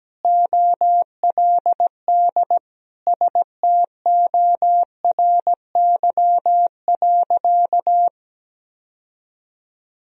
Morse code, heard as a keyed tone, about 17 wpm, 700 Hz. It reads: OLD STORY.
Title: Sentences from Top 1000 Words - Review Part 1 - 17wpm